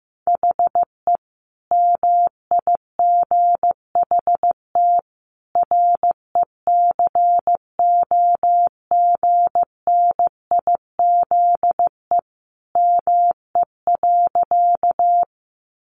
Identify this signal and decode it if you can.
Morse code, heard as a keyed tone, about 15 wpm, 700 Hz. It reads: HE MIGHT RECOGNIZE ME.
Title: Sentences from Top 1400 Words 15wpm